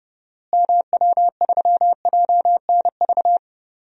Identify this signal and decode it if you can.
Morse code, heard as a keyed tone, about 30 wpm, 700 Hz. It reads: MW3JNV
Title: DXCC-100 Call Signs - 6 Characters - Repeated 3x - 30wpm